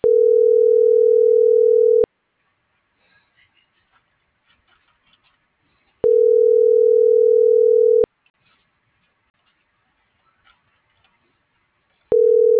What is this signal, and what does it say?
Ringback tone (North America), 440 + 480 Hz.